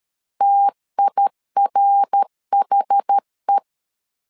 Morse code, heard as a keyed tone, about 13 wpm, 800 Hz. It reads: TIRHE
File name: CW / DET / 1 3 1500C S B 12wpm 2828